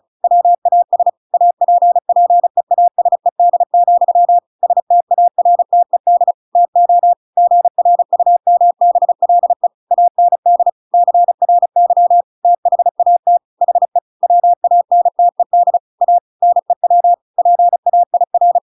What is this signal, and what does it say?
Morse code, heard as a keyed tone, about 34 wpm, 700 Hz. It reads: WAS APPEASED, STARTED TO GRUMBLE AND CRY THAT HE WANTED A NEW PAIR
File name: Pinocchio / Pinocchio ch8 35 35 ik1hge